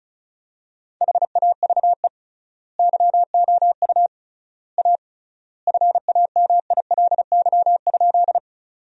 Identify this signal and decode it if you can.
Morse code, heard as a keyed tone, about 34 wpm, 700 Hz. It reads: HAVE YOU A FAMILY?
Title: 3x Word-Spacing - Sentences from Top 400 Words - 35wpm